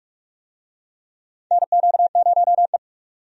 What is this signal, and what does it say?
Morse code, heard as a keyed tone, about 42 wpm, 700 Hz. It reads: NQ0E
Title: USA Call Signs - 4 Characters - 45wpm